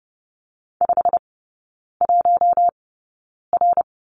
Morse code, heard as a keyed tone, about 30 wpm, spaced as for 9 wpm, 690 Hz. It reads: 51R
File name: Lesson 16e - Letter c - Three Characters 30wpm